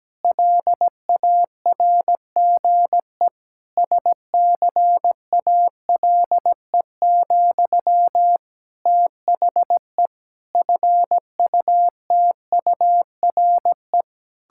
Morse code, heard as a keyed tone, about 17 wpm, 700 Hz. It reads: LARGE SCALE, THE FUTURE
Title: Sentences from Top 1000 Words - Review Part 5 - 17wpm